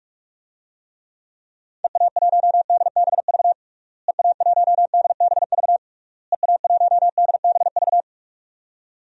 Morse code, heard as a keyed tone, about 41 wpm, 700 Hz. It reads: EA1DBV EA1DBV EA1DBV
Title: DXCC-100 Call Signs - 6 Characters - Repeated 3x - 45wpm